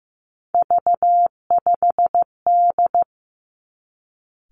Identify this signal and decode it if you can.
Morse code, heard as a keyed tone, about 15 wpm, 700 Hz. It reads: V5D
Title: Three Challenging Letter-Number Combinations 15wpm